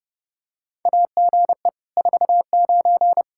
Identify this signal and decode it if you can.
Morse code, heard as a keyed tone, about 32 wpm, 700 Hz. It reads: AGE 49
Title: Practice QSO - Exchange 3 - More Info - 30wpm